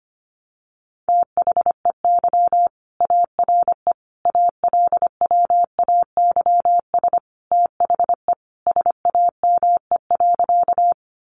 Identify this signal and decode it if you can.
Morse code, heard as a keyed tone, about 25 wpm, 700 Hz. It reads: THEY ARE ALWAYS THE SAME.
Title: Sentences from Top 300 Words 25wpm